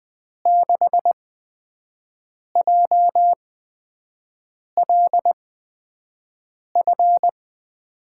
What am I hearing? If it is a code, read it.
Morse code, heard as a keyed tone, about 20 wpm, spaced as for 5 wpm, 700 Hz. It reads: 6JLF
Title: USA Call Signs - 5 Characters - 8x Character Spacing - 20wpm